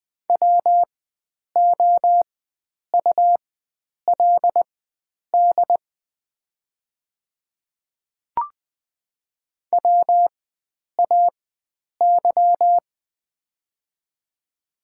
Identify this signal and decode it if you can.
Morse code, heard as a keyed tone, about 21 wpm, spaced as for 9 wpm, 700 Hz. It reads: WOULD WAY
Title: 4x Character Spacing - Top 100 Words - 20wpm